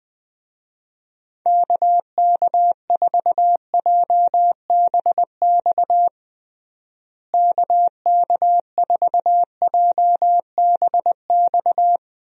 Morse code, heard as a keyed tone, about 21 wpm, 700 Hz. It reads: KK4JBX KK4JBX
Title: USA Call Signs - 6 Characters - Repeated 2x - 20wpm